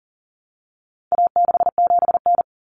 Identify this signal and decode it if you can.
Morse code, heard as a keyed tone, about 40 wpm, 700 Hz. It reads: A67N
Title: Common Cut Numbers - 4-Digit 40wpm